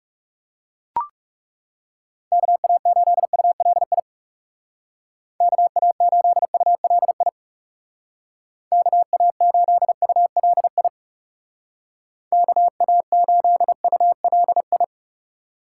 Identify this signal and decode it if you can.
Morse code, heard as a keyed tone, about 36 wpm, 700 Hz. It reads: KA8ULI KA8ULI KA8ULI KA8ULI
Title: Speed Racing - USA 2x3 Call Signs 30wpm